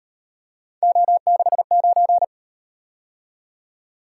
Morse code, heard as a keyed tone, about 38 wpm, 700 Hz. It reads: O69